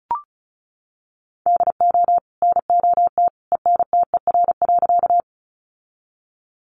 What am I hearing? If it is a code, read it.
Morse code, heard as a keyed tone, about 34 wpm, 700 Hz. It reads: DO NOT ENTER.
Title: Sentences from Top 1300 Words 35wpm